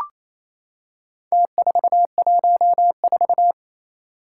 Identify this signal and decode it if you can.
Morse code, heard as a keyed tone, about 28 wpm, 700 Hz. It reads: T414
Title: Four Challenging Letter-Number Combinations 28wpm